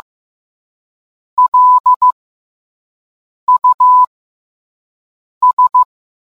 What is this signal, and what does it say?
Morse code, heard as a keyed tone, about 15 wpm, spaced as for 5 wpm, 1000 Hz. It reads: LUS